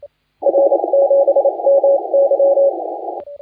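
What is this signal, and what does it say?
Morse code, heard as a keyed tone, about 29 wpm, 560 Hz, with noise. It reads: R 7 M Y
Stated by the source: Morse contest audio recorded by LA4C during the SAC CW 2018 contest.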